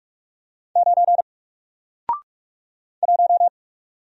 Morse code, heard as a keyed tone, about 48 wpm, 700 Hz. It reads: 9 1
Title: Single Character (Letter:Number:Punctuation:Prosign) Alt 45wpm